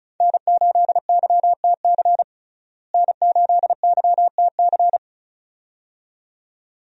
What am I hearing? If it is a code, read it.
Morse code, heard as a keyed tone, about 34 wpm, 700 Hz. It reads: N8YTC N8YTC
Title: USA Call Signs - 5 Characters - Repeated 2x - 35wpm